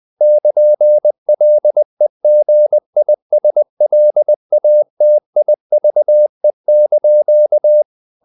Morse code, heard as a keyed tone, about 20 wpm, 600 Hz. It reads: <KN>LEGISLATIVE)
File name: CW / Practice / NationalCapitals20